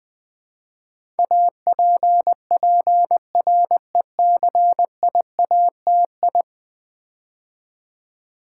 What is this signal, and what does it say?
Morse code, heard as a keyed tone, about 20 wpm, 700 Hz. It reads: APPRECIATI
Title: Build A Word - 5 Letters At A Time - 12-Letter Words - 20wpm